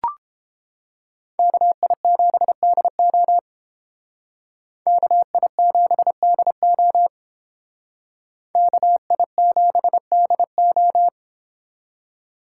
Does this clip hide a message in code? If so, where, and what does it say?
Yes: Morse code from 1.4 to 11.1 s, heard as a keyed tone, about 30 wpm, 700 Hz. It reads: KI7DO KI7DO KI7DO